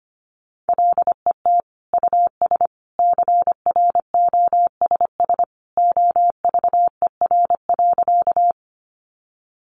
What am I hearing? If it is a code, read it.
Morse code, heard as a keyed tone, about 25 wpm, 700 Hz. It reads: LET US CROSS OVER.
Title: Sentences from Top 300 Words 25wpm